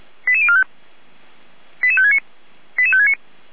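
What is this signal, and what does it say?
Five-tone selective call (ZVEI), heard as quick runs of five tones, tones of about 70 ms. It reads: 88045 88569 88569
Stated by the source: ZVEI five-tone call received on an SDR.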